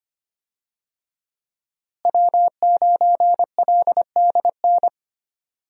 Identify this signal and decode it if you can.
Morse code, heard as a keyed tone, about 25 wpm, 700 Hz. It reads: W9LDN
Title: USA Call Signs - 5 Characters - Repeated 3x - 25wpm